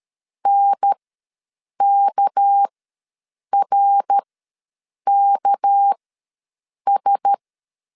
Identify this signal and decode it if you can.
Morse code, heard as a keyed tone, about 13 wpm, spaced as for 7 wpm, 800 Hz. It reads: NKRKS
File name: LEZIONI / 6 PKX0 / 6 PK ws 12wpm 3116